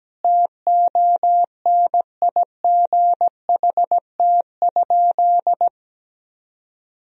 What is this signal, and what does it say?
Morse code, heard as a keyed tone, about 17 wpm, 700 Hz. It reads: TONIGHT?
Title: Sentences from Top 1000 Words - Review Part 2 - 17wpm